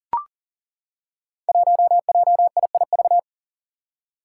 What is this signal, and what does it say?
Morse code, heard as a keyed tone, about 40 wpm, 700 Hz. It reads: 1JIIV
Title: Five Challenging Letter-Number Combinations 40wpm